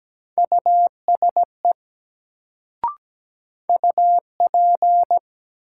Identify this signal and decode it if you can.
Morse code, heard as a keyed tone, about 17 wpm, 700 Hz. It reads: USE UP
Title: Sets of 2 Words - New Word Drill - 17wpm